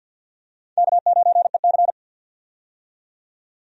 Morse code, heard as a keyed tone, about 50 wpm, 700 Hz. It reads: K9EC
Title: USA Call Signs - 4 Characters - 50wpm